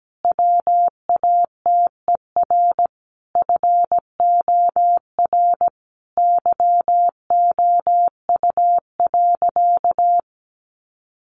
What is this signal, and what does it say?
Morse code, heard as a keyed tone, about 17 wpm, 700 Hz. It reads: WATER FOR YOU.
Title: Sentences from Top 1300 Words 17wpm